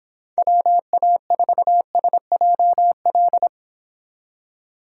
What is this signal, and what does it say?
Morse code, heard as a keyed tone, about 26 wpm, 700 Hz. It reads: WA4SJL